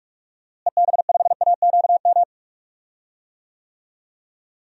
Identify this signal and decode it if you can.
Morse code, heard as a keyed tone, about 45 wpm, 700 Hz. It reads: EB5AQM
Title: DXCC-100 Call Signs - 6 Characters - Repeated 3x - 45wpm